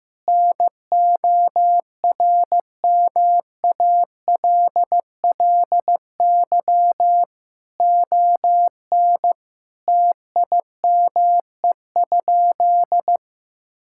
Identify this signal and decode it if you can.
Morse code, heard as a keyed tone, about 15 wpm, 700 Hz. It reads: NORMALLY ON TIME?
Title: Sentences from Top 1000 Words - Review Part 6 - 15wpm